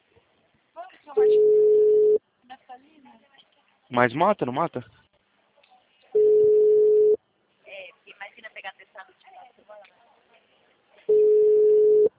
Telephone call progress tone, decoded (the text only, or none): ringback tone